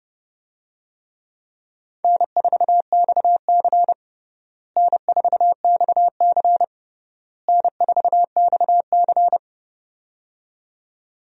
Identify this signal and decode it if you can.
Morse code, heard as a keyed tone, about 30 wpm, 700 Hz. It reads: N4XC N4XC N4XC
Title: USA Call Signs - 4 Characters - Repeated 3x - 30wpm